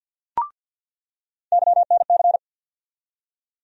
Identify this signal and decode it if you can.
Morse code, heard as a keyed tone, about 47 wpm, 700 Hz. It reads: YNC
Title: USA Call Sign Postfixes - 3 Letters 50wpm